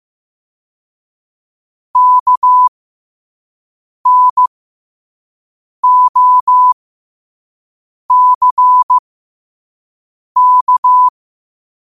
Morse code, heard as a keyed tone, about 15 wpm, spaced as for 5 wpm, 1000 Hz. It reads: KNOCK
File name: qotdfiles / 5wpm